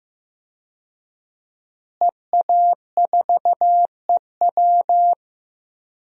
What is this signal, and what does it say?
Morse code, heard as a keyed tone, about 15 wpm, 700 Hz. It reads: EA4EW
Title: DXCC-100 Call Signs - 5 Characters - Repeated 2x - 15wpm